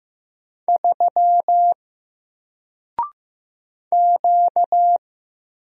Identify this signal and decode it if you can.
Morse code, heard as a keyed tone, about 15 wpm, 700 Hz. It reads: 3 Q
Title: Single Letter-Number Alt 15wpm